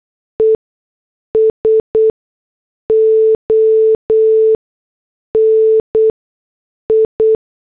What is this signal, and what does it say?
Morse code, heard as a keyed tone, about 8 wpm, spaced as for 6 wpm, 440 Hz. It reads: ESONI